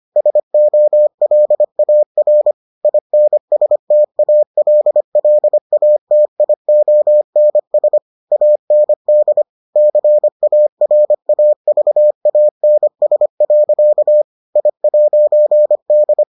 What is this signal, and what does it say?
Morse code, heard as a keyed tone, about 25 wpm, 600 Hz. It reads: SOLAR INSTALLATIONS AND CARAVANS. I'D